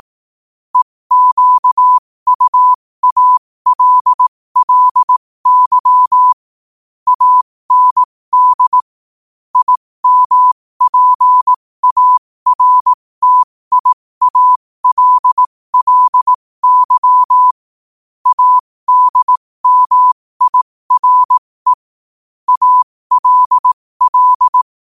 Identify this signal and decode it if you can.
Morse code, heard as a keyed tone, about 18 wpm, spaced as for 15 wpm, 1000 Hz. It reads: EQUALLY AND IMPARTIALLY ADMIRE ALL
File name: qotdfiles / 15wpm